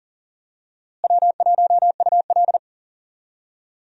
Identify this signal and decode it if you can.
Morse code, heard as a keyed tone, about 39 wpm, 700 Hz. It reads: W1UL
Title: USA Call Signs - 4 Characters - Repeated 2x - 40wpm